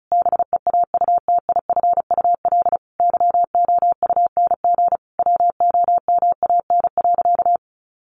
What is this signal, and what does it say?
Morse code, heard as a keyed tone, about 34 wpm, 700 Hz. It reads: BEAUTIFUL YOUNG WOMAN.
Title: Sentences from Top 1100 Words 35wpm